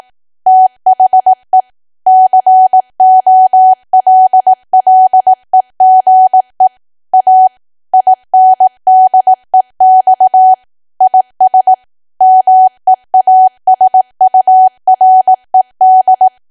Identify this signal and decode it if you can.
Morse code, heard as a keyed tone, about 18 wpm, 750 Hz. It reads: THE COLLEGE A INDEX IS MEASURED
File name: cwa / W1AW-Level2 / 18WPM